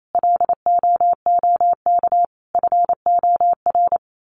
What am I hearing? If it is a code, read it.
Morse code, heard as a keyed tone, about 28 wpm, 700 Hz. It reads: LOOK FOR
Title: Sentences from Top 1100 Words 28wpm